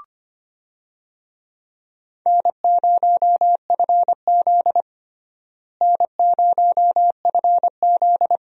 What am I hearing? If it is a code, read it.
Morse code, heard as a keyed tone, about 25 wpm, 700 Hz. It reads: N0FZ N0FZ
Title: USA Call Signs - 4 Characters - Repeated 2x - 25wpm